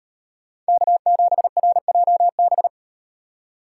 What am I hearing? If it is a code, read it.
Morse code, heard as a keyed tone, about 38 wpm, 700 Hz. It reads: K7RJB